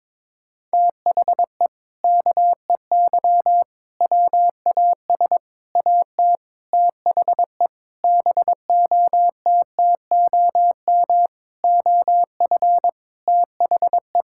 Morse code, heard as a keyed tone, about 22 wpm, 700 Hz. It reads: THE KEY WAS AT THE BOTTOM OF THE
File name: practice / 22wpm / Sentences with 4-Word Phrases - 22wpm